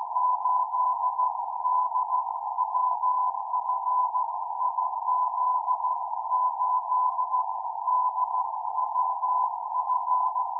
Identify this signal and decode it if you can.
Morse code, heard as a keyed tone, about 14 wpm, 1000 Hz, with noise. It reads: 8DWF IZ8DWF
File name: iw0ffk / 2008 05 23 IZ8DWF CQ